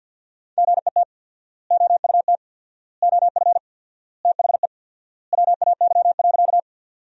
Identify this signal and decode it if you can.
Morse code, heard as a keyed tone, about 50 wpm, 700 Hz. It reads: GET OUT OF THE WAY.
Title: 4x Word-Spacing - Sentences from Top 200 Words - 50wpm